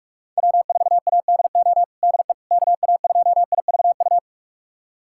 Morse code, heard as a keyed tone, about 46 wpm, 700 Hz. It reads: W4ADO DE KA2IVU
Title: USA Callsign DE Callsign 45wpm